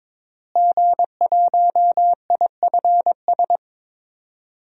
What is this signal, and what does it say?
Morse code, heard as a keyed tone, about 22 wpm, 700 Hz. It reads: G1IFS
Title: DXCC-100 Call Signs - 5 Characters - Repeated 3x - 22wpm